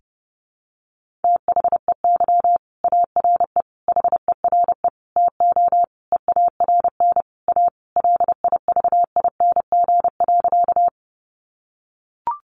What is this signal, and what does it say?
Morse code, heard as a keyed tone, about 30 wpm, 700 Hz. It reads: THEY ARE HERE TO EARN A LIVING.
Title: Sentences from Top 1500 Words 30wpm